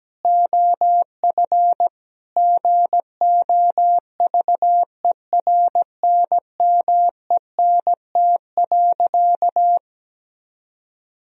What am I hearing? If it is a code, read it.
Morse code, heard as a keyed tone, about 17 wpm, 700 Hz. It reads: OF GOVERNMENT.
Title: Sentences from Top 1000 Words - Review Part 3 - 17wpm